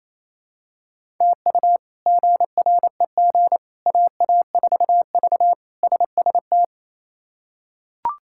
Morse code, heard as a keyed tone, about 28 wpm, 700 Hz. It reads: TU GREG AA4V SST
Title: K1USN SST QSO Practice - 28wpm